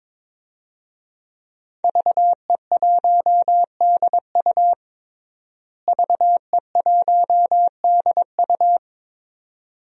Morse code, heard as a keyed tone, about 22 wpm, 700 Hz. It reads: VE1DU VE1DU
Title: DXCC-100 Call Signs - 5 Characters - Repeated 2x - 22wpm